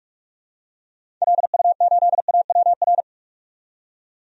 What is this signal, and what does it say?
Morse code, heard as a keyed tone, about 46 wpm, 700 Hz. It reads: LU8AWR
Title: DXCC-100 Call Signs - 6 Characters - 45wpm